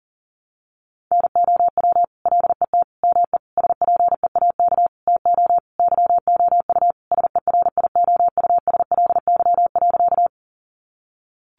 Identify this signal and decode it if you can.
Morse code, heard as a keyed tone, about 40 wpm, 700 Hz. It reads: NOW LET ME SPEAK TO YOU SERIOUSLY.
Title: Sentences from Top 1200 Words 40wpm